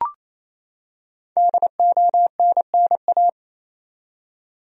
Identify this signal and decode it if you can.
Morse code, heard as a keyed tone, about 27 wpm, 700 Hz. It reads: DONNA
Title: USA Common Names - 28wpm